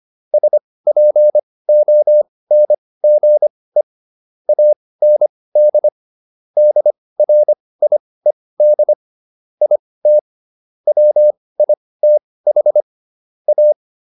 Morse code, heard as a keyed tone, about 25 wpm, spaced as for 18 wpm, 600 Hz. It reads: SPONGE AND DRIED IT WITH A